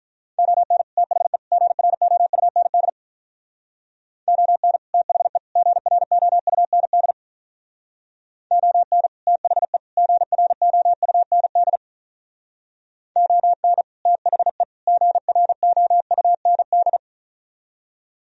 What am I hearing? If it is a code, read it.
Morse code, heard as a keyed tone, about 44 wpm, 700 Hz. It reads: ON THE GROUND ON THE GROUND ON THE GROUND ON THE GROUND